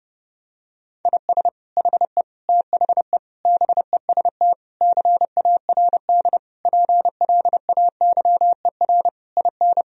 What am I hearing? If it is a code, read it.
Morse code, heard as a keyed tone, about 30 wpm, 700 Hz. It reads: IS HE THE BEST CARD PLAYER IN
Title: Sentences from Top 1000 Words - Review Part 4 - 30wpm